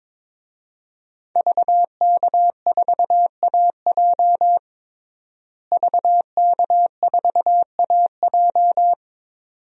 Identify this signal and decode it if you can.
Morse code, heard as a keyed tone, about 22 wpm, 700 Hz. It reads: VK4AJ VK4AJ
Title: DXCC-100 Call Signs - 5 Characters - Repeated 2x - 22wpm